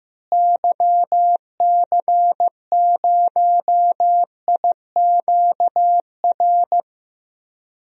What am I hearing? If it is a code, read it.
Morse code, heard as a keyed tone, about 15 wpm, 700 Hz. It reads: YC0IQR
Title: DXCC-100 Call Signs - 6 Characters - Repeated 2x - 15wpm